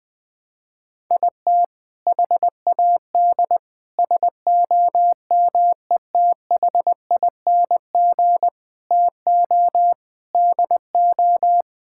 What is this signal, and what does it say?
Morse code, heard as a keyed tone, about 20 wpm, 700 Hz. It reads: IT HAD SOMETHING TO DO
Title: Sentences with 5-Word Phrases - 20wpm